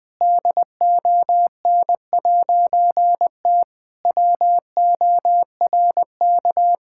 Morse code, heard as a keyed tone, about 20 wpm, 700 Hz. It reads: DON'T WORK
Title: All-Age Appropriate Jokes - 20wpm